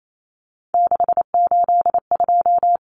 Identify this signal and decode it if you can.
Morse code, heard as a keyed tone, about 28 wpm, 700 Hz. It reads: 682
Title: Common Cut Numbers - 3-Digit 28wpm